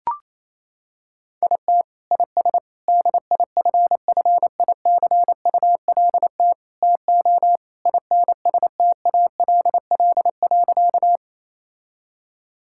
Morse code, heard as a keyed tone, about 28 wpm, 700 Hz. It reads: IT IS DIFFICULT TO INSTALL.
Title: Sentences from Top 1600 Words 28wpm